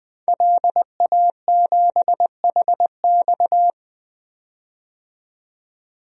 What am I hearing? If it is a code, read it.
Morse code, heard as a keyed tone, about 20 wpm, 700 Hz. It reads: LA7HX
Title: DXCC-100 Call Signs - 5 Characters - 20wpm